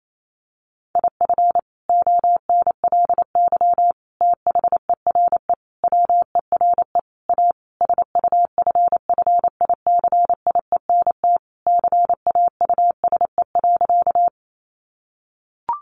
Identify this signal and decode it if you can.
Morse code, heard as a keyed tone, about 28 wpm, 700 Hz. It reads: IF ONLY THERE WERE A SUFFICIENT CAUSE.
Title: Sentences from Top 1400 Words 28wpm